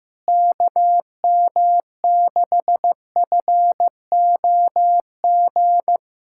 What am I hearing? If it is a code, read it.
Morse code, heard as a keyed tone, about 15 wpm, 700 Hz. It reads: KM6FOG
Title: USA Callsign DE Callsign - 6 Characters - 15wpm